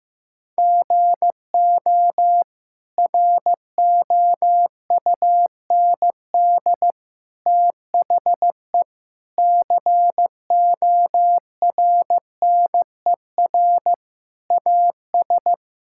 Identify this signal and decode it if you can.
Morse code, heard as a keyed tone, about 15 wpm, 700 Hz. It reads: GO ROUND THE CORNER AS
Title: Sentences from Top 1600 Words 15wpm